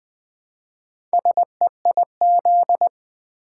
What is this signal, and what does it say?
Morse code, heard as a keyed tone, about 20 wpm, 700 Hz. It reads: SEIZ
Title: Build A Word - 1 Letter At A Time - 5-Letter Words - 20wpm